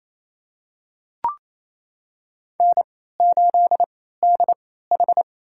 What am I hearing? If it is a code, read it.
Morse code, heard as a keyed tone, about 28 wpm, spaced as for 16 wpm, 700 Hz. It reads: N8DH
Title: USA Call Signs - 4 Characters - 3x Character Spacing - 28wpm